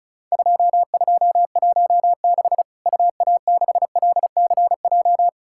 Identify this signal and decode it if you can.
Morse code, heard as a keyed tone, about 34 wpm, 700 Hz. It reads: 2216 UA6LCJ
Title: Worked All Europe DX Contest - QTC Exchange - 35wpm